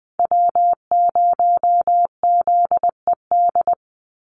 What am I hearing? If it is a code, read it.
Morse code, heard as a keyed tone, about 20 wpm, 700 Hz. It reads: W0ZED
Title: USA Call Signs 20wpm